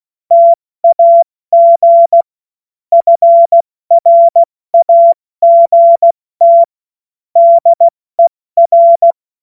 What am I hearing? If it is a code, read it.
Morse code, heard as a keyed tone, about 16 wpm, spaced as for 14 wpm, 680 Hz. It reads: TAG FRAGT DER
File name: Witz-16-September-2025-70-F80